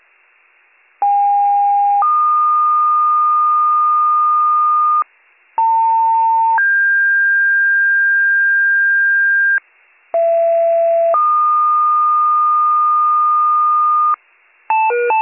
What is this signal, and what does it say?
Two-tone page, heard as a one-second tone, then a three-second tone, about 810 then 1220 Hz, about 900 then 1600 Hz, about 670 then 1220 Hz. It is heard from 1.0 to 14.2 s.